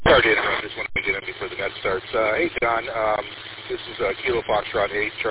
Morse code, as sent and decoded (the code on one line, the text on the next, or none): none